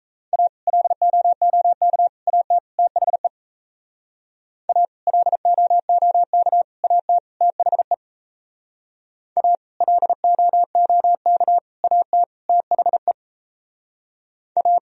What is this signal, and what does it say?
Morse code, heard as a keyed tone, about 35 wpm, 700 Hz. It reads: A LOOK AT THE A LOOK AT THE A LOOK AT THE A